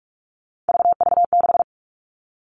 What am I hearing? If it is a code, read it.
Morse code, heard as a keyed tone, about 44 wpm, 700 Hz. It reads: VV6
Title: All Cut Numbers - 3-Digit 45wpm